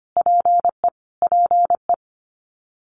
Morse code, heard as a keyed tone, about 25 wpm, 700 Hz. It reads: PE PE
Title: Canadian Provinces and Territories - Repeated 3x - 25wpm